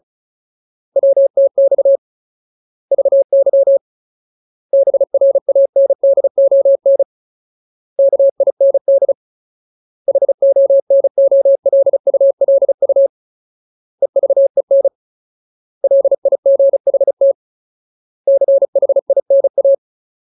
Morse code, heard as a keyed tone, about 34 wpm, 550 Hz. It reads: WTX VY BRANDON KIND HONOLULU EVEN LIGHT CHINA
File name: Ultimate Word List / 35WPM Ultimate